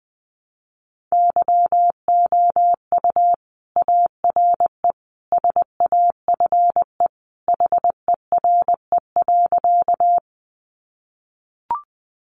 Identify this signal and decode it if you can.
Morse code, heard as a keyed tone, about 20 wpm, 700 Hz. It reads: YOU ARE SAFE HERE.
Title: Sentences from Top 900 Words 20wpm